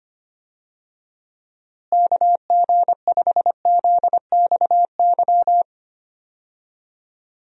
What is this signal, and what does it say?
Morse code, heard as a keyed tone, about 25 wpm, 700 Hz. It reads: KG5ZXY
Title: USA Call Signs - 6 Characters - Repeated 2x - 25wpm